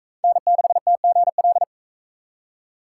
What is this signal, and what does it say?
Morse code, heard as a keyed tone, about 37 wpm, 700 Hz. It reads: N6TGL